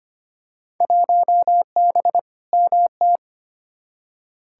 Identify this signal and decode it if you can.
Morse code, heard as a keyed tone, about 25 wpm, 700 Hz. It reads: 1B MT
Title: Field Day Exchanges - 25wpm